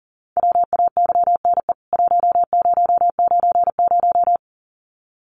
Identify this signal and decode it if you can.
Morse code, heard as a keyed tone, about 38 wpm, 700 Hz. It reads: WAYNE 1090
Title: CWT Exchange 40wpm